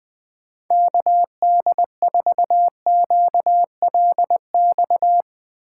Morse code, heard as a keyed tone, about 20 wpm, 700 Hz. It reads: KD4QLX